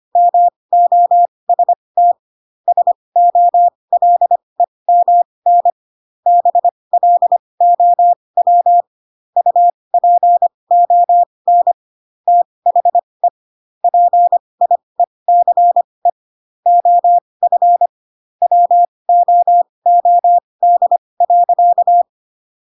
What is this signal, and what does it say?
Morse code, heard as a keyed tone, about 25 wpm, spaced as for 20 wpm, 700 Hz. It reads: MOST SOLEMN BLOW UPON THE PIECE OF WOOD.
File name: Pinocchio / Pinocchio ch1 15 25 ik1hge